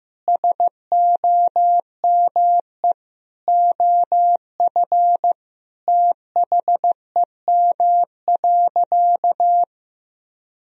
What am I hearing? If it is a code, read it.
Morse code, heard as a keyed tone, about 15 wpm, 700 Hz. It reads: SOME OF THEM.
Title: Sentences from Top 1000 Words - Review Part 3 - 15wpm